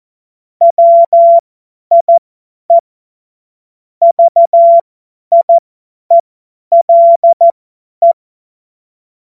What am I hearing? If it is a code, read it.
Morse code, heard as a keyed tone, about 14 wpm, spaced as for 10 wpm, 680 Hz. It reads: WIE VIELE
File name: Witz-16-September-2025-50-F70